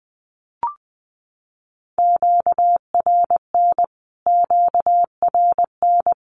Morse code, heard as a keyed tone, about 20 wpm, 700 Hz. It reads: QRN QRN
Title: Q-Codes Repeated 3x 20wpm